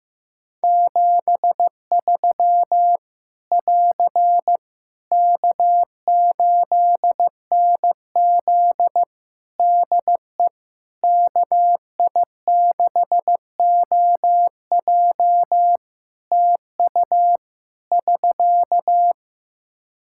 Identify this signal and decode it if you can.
Morse code, heard as a keyed tone, about 15 wpm, 700 Hz. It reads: 73 <AR> K8NZ DE KI6OJ TU <SK>
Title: Practice QSO - Exchange 4 - Ending - 15wpm